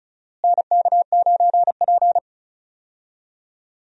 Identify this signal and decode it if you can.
Morse code, heard as a keyed tone, about 36 wpm, 700 Hz. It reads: NK9P
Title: USA Call Signs - 4 Characters - Repeated 2x - 35wpm